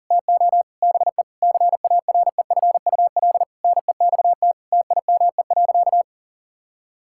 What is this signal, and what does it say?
Morse code, heard as a keyed tone, about 40 wpm, 700 Hz. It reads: TO BE CAREFUL NEXT TIME.
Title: Sentences from Top 1600 Words 40wpm